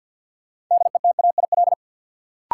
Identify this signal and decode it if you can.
Morse code, heard as a keyed tone, about 53 wpm, 700 Hz. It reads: DETAIL
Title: Sentences from Top 1500 Words - New Word Drill Alt - 50wpm